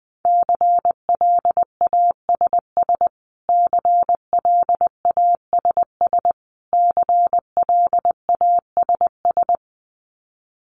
Morse code, heard as a keyed tone, about 20 wpm, 700 Hz. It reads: CLASS CLASS CLASS
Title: Top 500 Words Repeated 20wpm